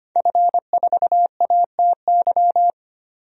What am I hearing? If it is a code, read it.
Morse code, heard as a keyed tone, about 25 wpm, 700 Hz. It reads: F4ATY
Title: DXCC-100 Call Signs - 5 Characters - 25wpm